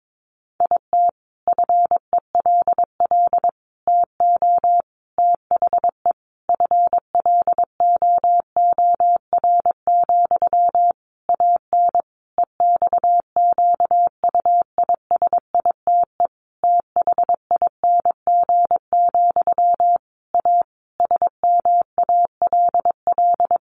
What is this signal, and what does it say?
Morse code, heard as a keyed tone, about 22 wpm, 700 Hz. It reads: IT FELL TO THE FLOOR, AN EXQUISITE THING, A SMALL
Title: A Sound Of Thunder 22wpm